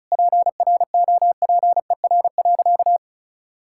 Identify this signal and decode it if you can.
Morse code, heard as a keyed tone, about 34 wpm, 700 Hz. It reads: PROPER.
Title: Sentences from Top 1000 Words - Review Part 5 - 35wpm